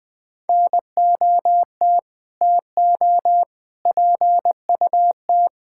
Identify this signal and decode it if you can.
Morse code, heard as a keyed tone, about 20 wpm, 700 Hz. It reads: NOT TO PUT
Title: Sets of 9 Words - 20wpm